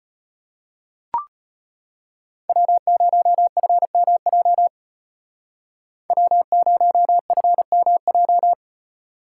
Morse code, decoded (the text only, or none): W0FMJ W0FMJ